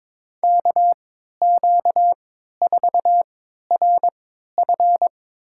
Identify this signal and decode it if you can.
Morse code, heard as a keyed tone, about 23 wpm, spaced as for 13 wpm, 700 Hz. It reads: KQ4RF
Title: USA Call Signs - 5 Characters - 3x Character Spacing - 22wpm